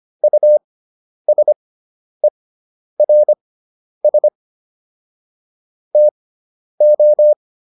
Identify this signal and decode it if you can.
Morse code, heard as a keyed tone, about 25 wpm, spaced as for 10 wpm, 600 Hz. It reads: USERS TO